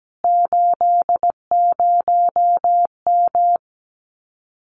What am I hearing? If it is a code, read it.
Morse code, heard as a keyed tone, about 17 wpm, 700 Hz. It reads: 80M
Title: Common QSO Phrases Encore 17wpm